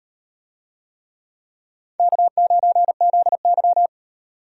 Morse code, heard as a keyed tone, about 38 wpm, 700 Hz. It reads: K9ZY